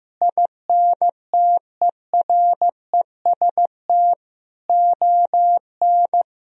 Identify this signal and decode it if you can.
Morse code, heard as a keyed tone, about 15 wpm, 700 Hz. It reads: INTEREST ON
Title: Sentences from Top 1000 Words - Review Part 4 - 15wpm